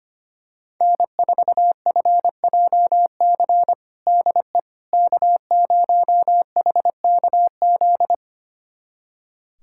Morse code, heard as a keyed tone, about 25 wpm, 700 Hz. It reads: N4FJC DE K0HKZ
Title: USA Callsign DE Callsign - 5 Characters - 25wpm